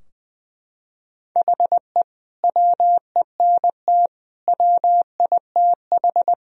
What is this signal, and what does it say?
Morse code, heard as a keyed tone, about 20 wpm, 700 Hz. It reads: HE WENT WITH
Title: Sentences from Top 1000 Words - Review Part 3 - 20wpm